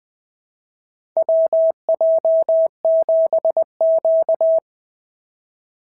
Morse code, heard as a keyed tone, about 20 wpm, 690 Hz. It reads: WJ7Q
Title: SOTA QSO Practice 20wpm